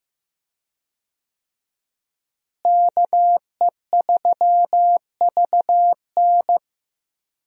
Morse code, heard as a keyed tone, about 15 wpm, 700 Hz. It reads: KE3VN
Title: USA Call Signs - 5 Characters - Repeated 3x - 15wpm